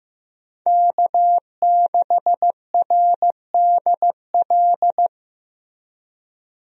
Morse code, heard as a keyed tone, about 15 wpm, 700 Hz. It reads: K6RDL
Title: USA Call Signs - 5 Characters - Repeated 3x - 15wpm